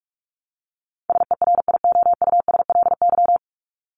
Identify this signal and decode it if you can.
Morse code, heard as a keyed tone, about 44 wpm, 700 Hz. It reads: SERIOUSLY
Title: Top 1001-1250 Words 45wpm